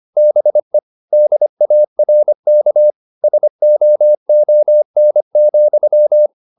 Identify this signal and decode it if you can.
Morse code, heard as a keyed tone, about 25 wpm, 600 Hz. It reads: BE DARK SOON,